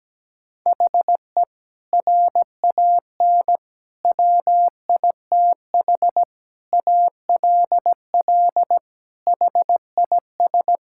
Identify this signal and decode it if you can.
Morse code, heard as a keyed tone, about 17 wpm, 700 Hz. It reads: HE RAN WITH ALL HIS
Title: Sentences from Top 1000 Words - Review Part 5 - 17wpm